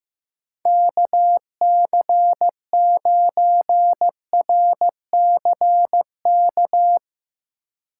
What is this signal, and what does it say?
Morse code, heard as a keyed tone, about 15 wpm, 700 Hz. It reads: KC9RCK
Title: USA Call Signs - 6 Characters - 3x Character Spacing - 15wpm